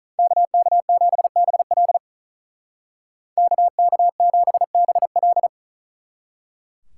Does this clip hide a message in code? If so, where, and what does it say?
Yes: Morse code from 0.2 to 5.5 s, heard as a keyed tone, about 47 wpm, 700 Hz. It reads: KK7BL KK7BL